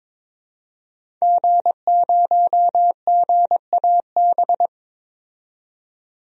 Morse code, heard as a keyed tone, about 22 wpm, 700 Hz. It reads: G0GAB
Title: DXCC-100 Call Signs - 5 Characters - 22wpm